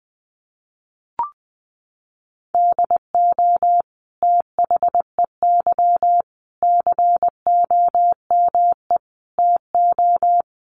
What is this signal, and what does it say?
Morse code, heard as a keyed tone, about 20 wpm, 700 Hz. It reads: DO THEY COME TO
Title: Sentences from Top 1500 Words 20wpm